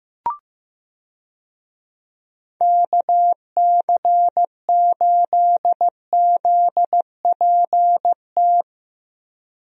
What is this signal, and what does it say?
Morse code, heard as a keyed tone, about 15 wpm, 700 Hz. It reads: KC8ZPT
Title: USA Call Signs - 6 Characters - Repeated 3x - 15wpm